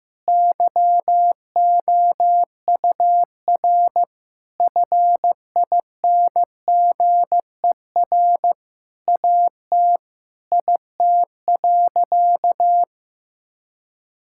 Morse code, heard as a keyed tone, about 15 wpm, 700 Hz. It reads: YOUR FINGER AT IT.
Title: Sentences from Top 1600 Words 15wpm